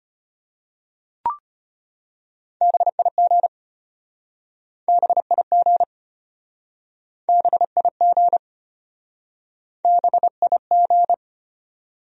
Morse code, heard as a keyed tone, about 34 wpm, 700 Hz. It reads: BIG BIG BIG BIG